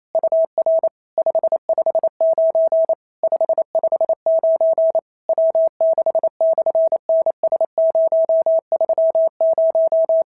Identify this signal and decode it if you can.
Morse code, heard as a keyed tone, about 28 wpm, 650 Hz. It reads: UR 559 559 W6/NS030
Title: SOTA QSO Practice 28wpm